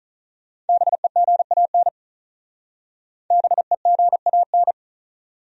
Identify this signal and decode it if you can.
Morse code, heard as a keyed tone, about 44 wpm, 700 Hz. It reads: BEGAN BEGAN